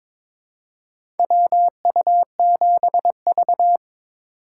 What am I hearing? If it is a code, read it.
Morse code, heard as a keyed tone, about 22 wpm, 700 Hz. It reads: WU7V
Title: USA Call Signs - 4 Characters - 4x Character Spacing - 22wpm